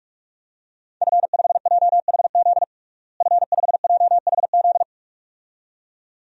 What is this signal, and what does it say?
Morse code, heard as a keyed tone, about 46 wpm, 700 Hz. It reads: F5JHZ F5JHZ
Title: DXCC-100 Call Signs - 5 Characters - Repeated 2x - 45wpm